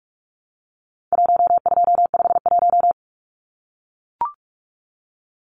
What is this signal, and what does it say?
Morse code, heard as a keyed tone, about 44 wpm, 700 Hz. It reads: 1251
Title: Numbers - Contest Style - 4-Digit 45wpm